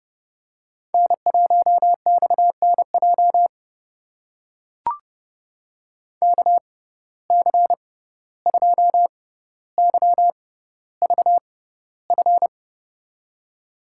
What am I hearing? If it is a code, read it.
Morse code, heard as a keyed tone, about 30 wpm, spaced as for 15 wpm, 700 Hz. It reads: N1XNJ KC2YVF
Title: USA Call Signs - 6x Character Spacing - 30wpm